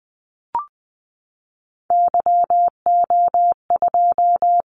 Morse code, heard as a keyed tone, about 20 wpm, 700 Hz. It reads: YO2
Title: Callsign Prefixes for Top 25 DXCC - 20wpm